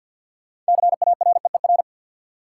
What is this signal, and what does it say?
Morse code, heard as a keyed tone, about 51 wpm, 700 Hz. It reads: CAREER
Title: Sentences from Top 1000 Words - New Word Drill - 50wpm